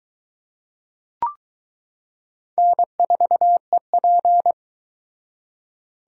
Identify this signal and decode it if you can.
Morse code, heard as a keyed tone, about 22 wpm, 700 Hz. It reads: N4EP